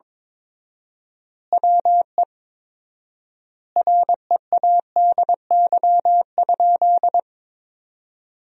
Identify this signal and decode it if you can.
Morse code, heard as a keyed tone, about 21 wpm, 700 Hz. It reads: WE READY?
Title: 4x Word-Spacing - Sentences from Top 400 Words - 22wpm